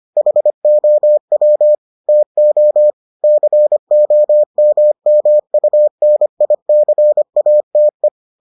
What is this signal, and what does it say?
Morse code, heard as a keyed tone, about 25 wpm, 600 Hz. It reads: HOW TO COMMUNICATE